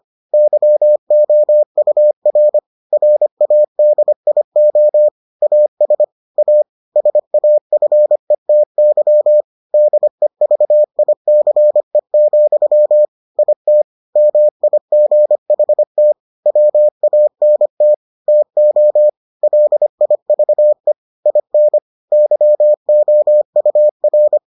Morse code, heard as a keyed tone, about 25 wpm, 600 Hz. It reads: YOUR RADIO AS A SAFETY DEVICE, IT MIGHT WANT TO LIVE IN YOUR